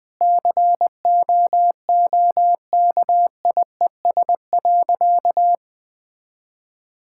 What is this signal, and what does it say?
Morse code, heard as a keyed tone, about 21 wpm, 700 Hz. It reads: COOKIES.
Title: Sentences with 5-Word Phrases - Spoken Beginning - 20wpm